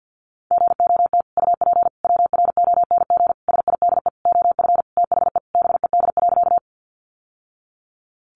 Morse code, heard as a keyed tone, about 50 wpm, 700 Hz. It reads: GOT UP WRONG SIDE OF THE BED.
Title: Sentences from Top 900 Words 50wpm